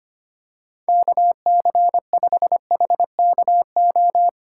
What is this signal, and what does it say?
Morse code, heard as a keyed tone, about 25 wpm, 700 Hz. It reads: KC5HKO
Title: USA Call Signs - 4x Character Spacing - 25wpm